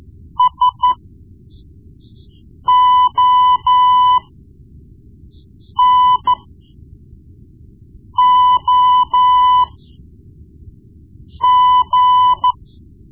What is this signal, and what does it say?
Morse code, heard as a keyed tone, about 13 wpm, spaced as for 4 wpm, 1000 Hz. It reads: SONOG
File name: Morse Lessons / Morse Lessons / Tape-1B